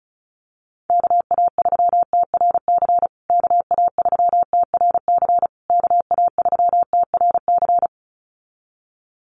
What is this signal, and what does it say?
Morse code, heard as a keyed tone, about 34 wpm, 700 Hz. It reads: KA3TRC KA3TRC KA3TRC
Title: USA Call Signs Repeated 3x 35wpm